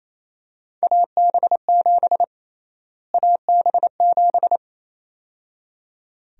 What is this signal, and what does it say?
Morse code, heard as a keyed tone, about 28 wpm, 700 Hz. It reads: AB7 AB7
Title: USA Call Sign Prefixes - Repeated 2x - 28wpm